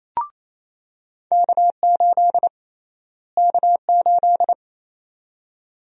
Morse code, heard as a keyed tone, about 28 wpm, 700 Hz. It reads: K8 K8
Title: USA Call Sign Prefixes - Repeated 2x - 28wpm